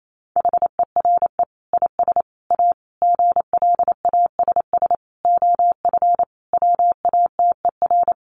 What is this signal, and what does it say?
Morse code, heard as a keyed tone, about 28 wpm, 700 Hz. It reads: HERE IS A GLASS OF WATER
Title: Sentences from Top 1300 Words 28wpm